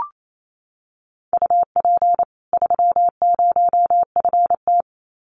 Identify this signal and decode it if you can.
Morse code, heard as a keyed tone, about 28 wpm, 700 Hz. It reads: UP 30FT
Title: Common QSO Phrases 28wpm